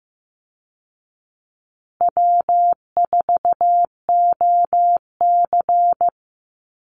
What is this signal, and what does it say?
Morse code, heard as a keyed tone, about 15 wpm, 700 Hz. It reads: W4OC
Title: Easy Call Signs Repeated 3x - 15wpm